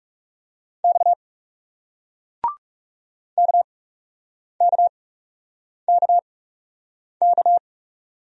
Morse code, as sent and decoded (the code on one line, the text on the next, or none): -..- / -.- -.- -.- -.-
X KKKK